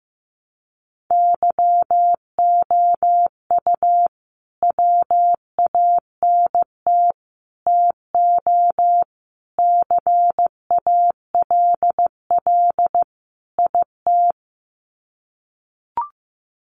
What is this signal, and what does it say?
Morse code, heard as a keyed tone, about 15 wpm, 700 Hz. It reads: YOU WANT TO CALL IT
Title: Sets of 5 Words Encore2 15wpm